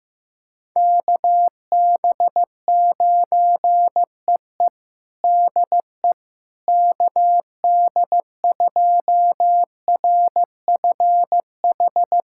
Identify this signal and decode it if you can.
Morse code, heard as a keyed tone, about 15 wpm, 700 Hz. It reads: KB9EE DE KD2RFH
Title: USA Callsign DE Callsign 15wpm